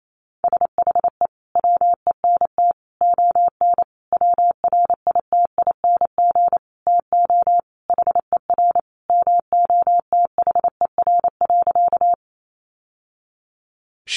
Morse code, heard as a keyed tone, about 28 wpm, 700 Hz. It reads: SHE WENT ON WRITING TO HER MOTHER.
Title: Sentences from Top 800 Words 28wpm